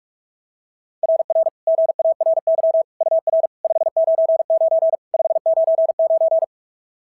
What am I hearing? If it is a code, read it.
Morse code, heard as a keyed tone, about 41 wpm, 690 Hz. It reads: RR GARY UR 599 599
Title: SOTA QSO Practice 45wpm